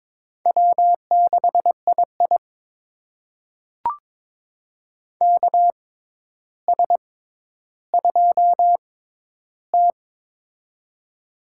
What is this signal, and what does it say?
Morse code, heard as a keyed tone, about 22 wpm, spaced as for 11 wpm, 700 Hz. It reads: W6II KS2T
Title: USA Call Signs - 6x Character Spacing - 22wpm